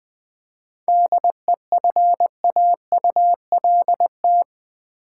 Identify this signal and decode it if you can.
Morse code, heard as a keyed tone, about 20 wpm, 700 Hz. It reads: DEFAULT
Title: 7 Letter Words - First 3 Letters Given - 20wpm